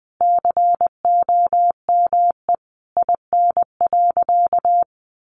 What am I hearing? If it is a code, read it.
Morse code, heard as a keyed tone, about 19 wpm, 700 Hz. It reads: COME IN.
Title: Sentences from Top 1000 Words 20wpm